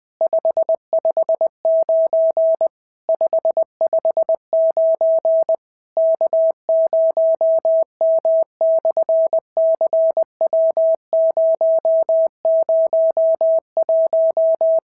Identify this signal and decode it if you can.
Morse code, heard as a keyed tone, about 20 wpm, 650 Hz. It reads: 559 559 K0M/CW001
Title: SOTA QSO Practice 20wpm